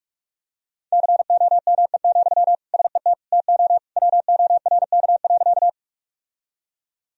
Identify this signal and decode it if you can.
Morse code, heard as a keyed tone, about 46 wpm, 700 Hz. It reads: COME, SET TO WORK.
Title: Sentences from Top 1000 Words - Review Part 1 - 45wpm